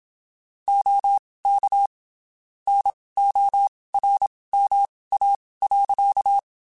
Morse code, heard as a keyed tone, about 27 wpm, spaced as for 19 wpm, 800 Hz. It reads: OK NORMA.
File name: (spaced) / QSOs Spaced-mp3s / WD0FBT 1 @25 sp